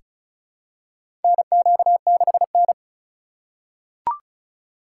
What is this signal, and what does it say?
Morse code, heard as a keyed tone, about 36 wpm, 700 Hz. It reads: NQ6N
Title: USA Call Signs - 4 Characters - 3x Character Spacing - 35wpm